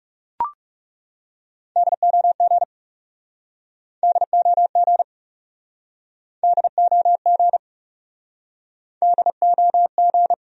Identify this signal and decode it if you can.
Morse code, heard as a keyed tone, about 42 wpm, 700 Hz. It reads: DOG DOG DOG DOG